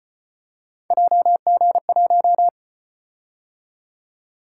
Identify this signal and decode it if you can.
Morse code, heard as a keyed tone, about 34 wpm, 700 Hz. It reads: JG1